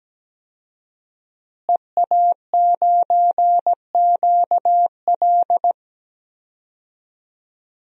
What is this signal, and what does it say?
Morse code, heard as a keyed tone, about 17 wpm, 700 Hz. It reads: EA9QL
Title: DXCC-100 Call Signs - 5 Characters - Repeated 3x - 17wpm